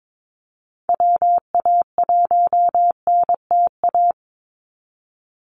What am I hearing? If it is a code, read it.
Morse code, heard as a keyed tone, about 22 wpm, 700 Hz. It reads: WA1NTA
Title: Call Signs Repeated 2x - 22wpm